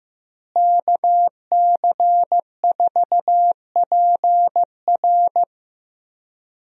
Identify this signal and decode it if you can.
Morse code, heard as a keyed tone, about 15 wpm, 700 Hz. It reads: KC4PR
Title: USA Call Signs - 5 Characters - Repeated 2x - 15wpm